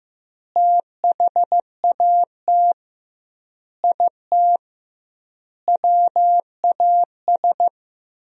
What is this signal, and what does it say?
Morse code, heard as a keyed tone, about 15 wpm, 700 Hz. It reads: THAT IT WAS
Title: 2x Word-Spacing - Sentences from Top 400 Words - 15wpm